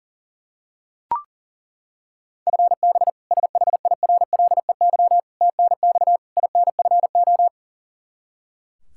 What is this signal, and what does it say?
Morse code, heard as a keyed tone, about 40 wpm, 700 Hz. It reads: FB SHIRLEY TNX INFO
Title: K1USN SST QSO Practice 40wpm